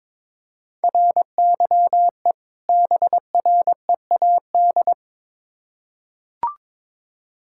Morse code, heard as a keyed tone, about 22 wpm, 700 Hz. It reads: RYE BREAD
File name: Difficult Words with Hints - 22wpm